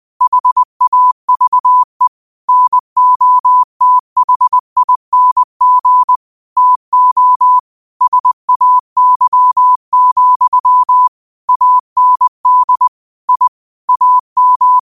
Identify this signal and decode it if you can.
Morse code, heard as a keyed tone, about 20 wpm, 1000 Hz. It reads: HAVE NOTHING TO SAY, AND I AM
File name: 20wpm